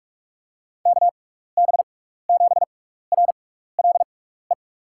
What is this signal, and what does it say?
Morse code, heard as a keyed tone, about 45 wpm, spaced as for 15 wpm, 700 Hz. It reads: KB7RLE
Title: USA Call Signs - 6 Characters - 6x Character Spacing - 45wpm